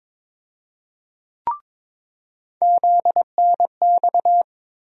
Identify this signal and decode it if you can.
Morse code, heard as a keyed tone, about 22 wpm, 700 Hz. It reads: ZNX
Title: USA Call Sign Postfixes - 3 Letters 22wpm